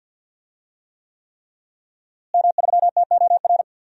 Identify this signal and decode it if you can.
Morse code, heard as a keyed tone, about 52 wpm, 700 Hz. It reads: M3TOR
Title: DXCC-100 Call Signs - 5 Characters - Repeated 3x - 50wpm